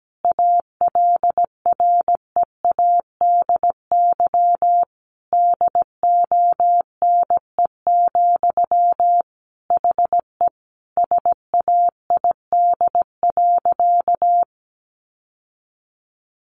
Morse code, heard as a keyed tone, about 17 wpm, 700 Hz. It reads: ALREADY DONE, HE SAID.
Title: Sentences from Top 700 Words 17wpm